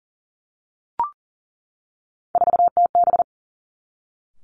Morse code, heard as a keyed tone, about 42 wpm, 700 Hz. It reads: 4TB
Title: Three Challenging Letter-Number Combinations 40wpm